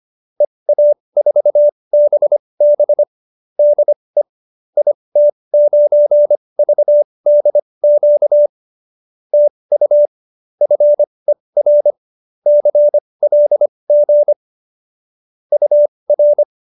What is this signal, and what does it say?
Morse code, heard as a keyed tone, about 25 wpm, spaced as for 20 wpm, 590 Hz. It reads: EA4BB DE IT9VDQ TU FER CLG UR
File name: qso215b 20